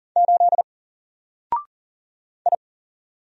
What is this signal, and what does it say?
Morse code, heard as a keyed tone, about 40 wpm, 700 Hz. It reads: 8 I
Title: Single Letter-Number Alt 40wpm